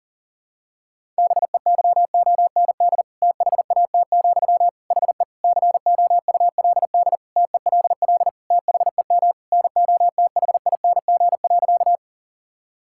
Morse code, heard as a keyed tone, about 40 wpm, 700 Hz. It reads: BEYOND THAT, HE COULD TELL THEM NOTHING.
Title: Sentences from Top 1000 Words - Review Part 6 - 40wpm